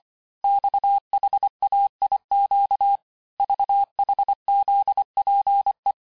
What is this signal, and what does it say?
Morse code, heard as a keyed tone, about 25 wpm, 800 Hz. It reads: XHAIQ VHZPE
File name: texte-chiffre-vitesse-23mpm